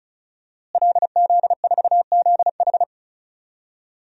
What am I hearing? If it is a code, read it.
Morse code, heard as a keyed tone, about 36 wpm, 700 Hz. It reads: LZ4ZH
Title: DXCC-100 Call Signs - 5 Characters - Repeated 3x - 35wpm